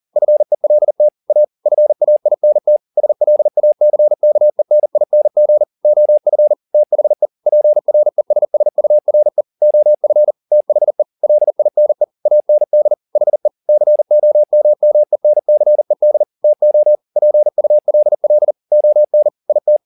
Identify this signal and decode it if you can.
Morse code, heard as a keyed tone, about 40 wpm, 600 Hz. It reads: FELT A FAINT SLACKENING OF THE PRESSURE OF THE LINE AND HE COMMENCED TO PULL ON IT
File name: the old man and the sea 2 40wpm